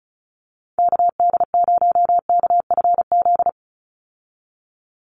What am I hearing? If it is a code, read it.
Morse code, heard as a keyed tone, about 34 wpm, 700 Hz. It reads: KD0KFZ
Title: USA Call Signs 35wpm